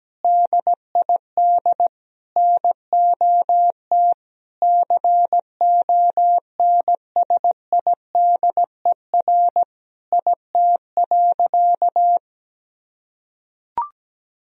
Morse code, heard as a keyed tone, about 17 wpm, 700 Hz. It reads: DID NOT CONSIDER IT.
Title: Sentences from Top 1000 Words - Review Part 4 - 17wpm